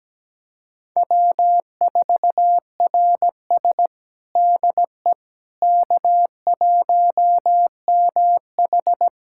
Morse code, heard as a keyed tone, about 17 wpm, 700 Hz. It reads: W4RS DE K1MH
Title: USA Callsign DE Callsign - 4 Characters - 17wpm